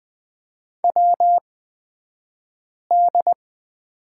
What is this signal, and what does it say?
Morse code, heard as a keyed tone, about 20 wpm, spaced as for 5 wpm, 700 Hz. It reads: WD
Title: 223 Lesson 36d - BT - Two Characters 20wpm